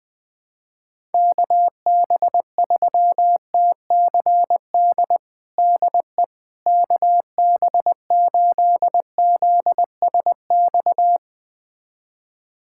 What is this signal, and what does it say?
Morse code, heard as a keyed tone, about 20 wpm, 700 Hz. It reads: KB3TCD DE KB8ZSX
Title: USA Callsign DE Callsign - 6 Characters - 20wpm